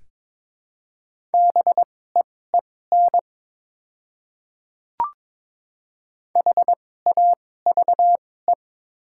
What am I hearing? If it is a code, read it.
Morse code, heard as a keyed tone, about 22 wpm, spaced as for 16 wpm, 700 Hz. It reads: BEEN HAVE
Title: 2x Character Spacing - Top 100 Words - 22wpm